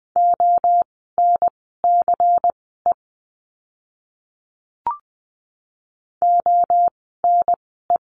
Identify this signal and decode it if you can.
Morse code, heard as a keyed tone, about 20 wpm, spaced as for 15 wpm, 700 Hz. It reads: ONCE ONE
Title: 108 Lesson 18f - Letter m - Words 20wpm